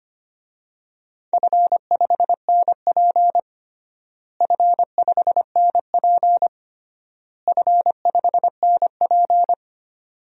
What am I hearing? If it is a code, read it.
Morse code, heard as a keyed tone, about 25 wpm, 700 Hz. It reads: F5NP F5NP F5NP
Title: DXCC-100 Call Signs - 4 Characters - Repeated 3x - 25wpm